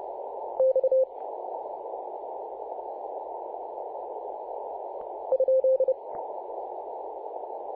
Morse code, heard as a keyed tone, about 30 wpm, 520 Hz, with noise. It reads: X ?
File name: c21gc-24-895mhz-cw